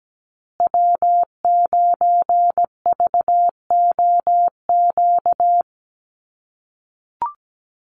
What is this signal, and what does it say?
Morse code, heard as a keyed tone, about 17 wpm, 700 Hz. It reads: W9VOQ
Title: USA Call Signs Repeated 3x 17wpm